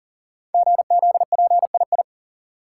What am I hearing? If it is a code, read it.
Morse code, heard as a keyed tone, about 40 wpm, 700 Hz. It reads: GZPII
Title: Code Groups - 5 Letter:Numbers - 40wpm